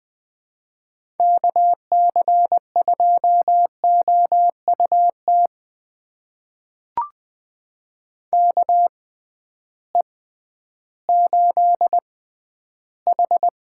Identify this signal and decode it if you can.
Morse code, heard as a keyed tone, about 20 wpm, spaced as for 12 wpm, 700 Hz. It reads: KC2OUT KE8H
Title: USA Call Signs - 6 Characters - 6x Character Spacing - 20wpm